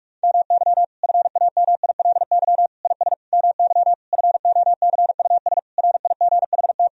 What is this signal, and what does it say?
Morse code, heard as a keyed tone, about 45 wpm, 700 Hz. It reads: MY FAMILY IS MY FOCUS RIGHT
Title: Sentences from Top 1000 Words - Review Part 4 - 45wpm